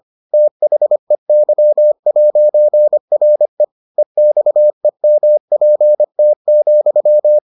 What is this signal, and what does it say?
Morse code, heard as a keyed tone, about 25 wpm, 600 Hz. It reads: THEY'RE EXEMPT,